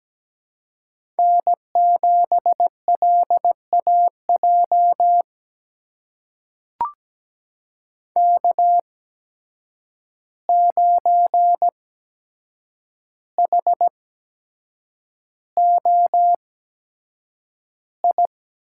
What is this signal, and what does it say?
Morse code, heard as a keyed tone, about 17 wpm, spaced as for 7 wpm, 700 Hz. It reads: N7LAJ K9HOI
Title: USA Call Signs - 5 Characters - 8x Character Spacing - 17wpm